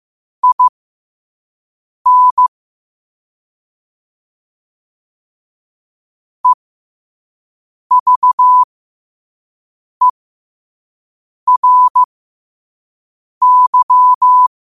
Morse code, heard as a keyed tone, about 15 wpm, spaced as for 5 wpm, 1000 Hz. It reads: IN EVERY
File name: qotdfiles / 5wpm